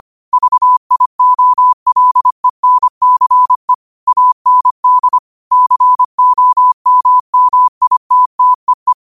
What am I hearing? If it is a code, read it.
Morse code, heard as a keyed tone, about 25 wpm, 1000 Hz. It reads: VIOLENCE AND COMMITTEE
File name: qotdfiles / 25wpm